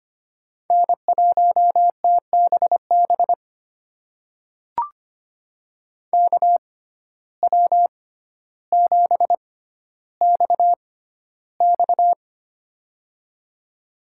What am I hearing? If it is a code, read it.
Morse code, heard as a keyed tone, about 25 wpm, spaced as for 13 wpm, 700 Hz. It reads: N1TBB KW7XX
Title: USA Call Signs - 5 Characters - 6x Character Spacing - 25wpm